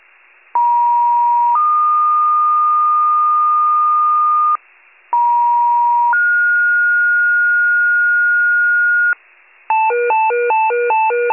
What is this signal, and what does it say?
Two-tone page, heard as a one-second tone, then a three-second tone, about 950 then 1220 Hz, about 950 then 1430 Hz, with noise. It is heard from 0.5 to 9.1 s.
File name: wcalert